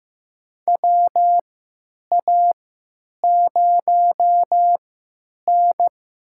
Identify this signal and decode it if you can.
Morse code, heard as a keyed tone, about 15 wpm, spaced as for 8 wpm, 700 Hz. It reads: WA0N
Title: USA Call Signs - 3x Character Spacing - 15wpm